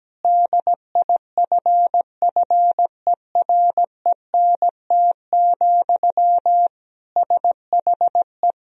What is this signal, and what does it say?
Morse code, heard as a keyed tone, about 17 wpm, 700 Hz. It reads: DIFFERENT, SHE
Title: Sentences from Top 1000 Words - Review Part 3 - 17wpm